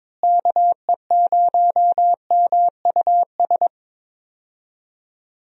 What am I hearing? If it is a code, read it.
Morse code, heard as a keyed tone, about 22 wpm, 700 Hz. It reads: KE0MUS
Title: USA Call Signs - 6 Characters - Repeated 3x - 22wpm